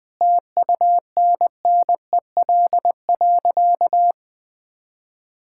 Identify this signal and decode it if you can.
Morse code, heard as a keyed tone, about 19 wpm, 700 Hz. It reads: TUNNEL.
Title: Sentences with 8-Word Phrases - 20wpm